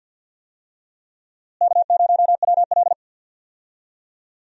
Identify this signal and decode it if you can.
Morse code, heard as a keyed tone, about 50 wpm, 700 Hz. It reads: K0WL
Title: USA Call Signs - 4 Characters - Repeated 3x - 50wpm